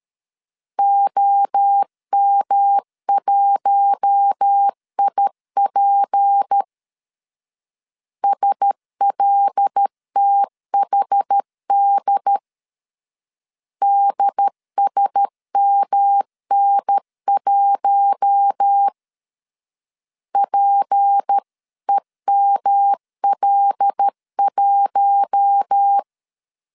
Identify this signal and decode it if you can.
Morse code, heard as a keyed tone, about 13 wpm, 800 Hz. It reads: OM1IP SLTHD DSMN1 PEML1
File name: LEZIONI / 6 PKX0 / 6 P 12wpm 2033